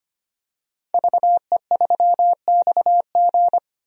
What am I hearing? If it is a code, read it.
Morse code, heard as a keyed tone, about 25 wpm, 700 Hz. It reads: VE3XG
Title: DXCC-100 Call Signs - 5 Characters - 25wpm